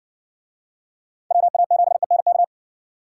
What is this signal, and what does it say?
Morse code, heard as a keyed tone, about 63 wpm, 700 Hz. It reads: WA7ENK